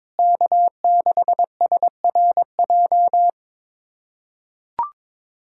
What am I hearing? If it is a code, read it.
Morse code, heard as a keyed tone, about 22 wpm, 700 Hz. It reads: K6SRJ
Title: USA Callsign DE Callsign - 5 Characters - 22wpm